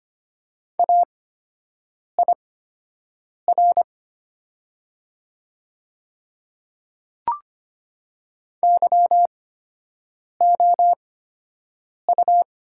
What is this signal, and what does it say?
Morse code, heard as a keyed tone, about 25 wpm, spaced as for 7 wpm, 700 Hz. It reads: AIR YOU
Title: 8x Character Spacing - Top 200 Words - 25wpm